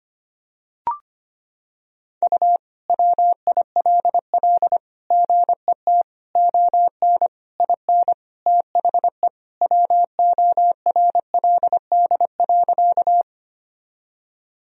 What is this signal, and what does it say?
Morse code, heard as a keyed tone, about 25 wpm, 700 Hz. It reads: U WILL GET ON IN THE WORLD.
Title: Sentences from Top 200 Words and Abbreviations 25wpm